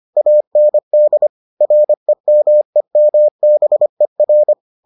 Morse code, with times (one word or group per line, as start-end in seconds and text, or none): 0.2-1.3 AND
1.6-4.5 REMEMBER